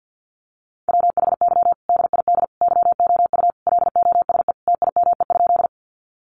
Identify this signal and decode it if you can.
Morse code, heard as a keyed tone, about 48 wpm, 700 Hz. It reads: WHY DID YOU LOSE TIME?
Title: Sentences from Top 1100 Words 50wpm